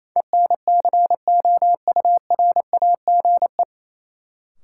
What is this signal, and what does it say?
Morse code, heard as a keyed tone, about 28 wpm, 700 Hz. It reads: ENCOURAGE
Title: Sentences from Top 1400 Words - New Word Drill Alt - 28wpm